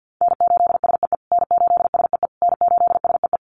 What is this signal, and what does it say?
Morse code, heard as a keyed tone, about 50 wpm, 700 Hz. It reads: N8SEE N8SEE N8SEE
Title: USA Call Signs Repeated 3x 50wpm